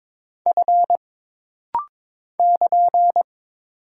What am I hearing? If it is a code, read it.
Morse code, heard as a keyed tone, about 22 wpm, 700 Hz. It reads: F <KN>
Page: This rapid-fire timing is geared to push your ICR (Instant Character Recognition) skills.